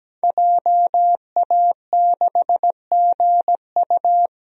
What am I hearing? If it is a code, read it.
Morse code, heard as a keyed tone, about 17 wpm, 700 Hz. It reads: JA6GU
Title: DXCC-100 Call Signs - 5 Characters - 17wpm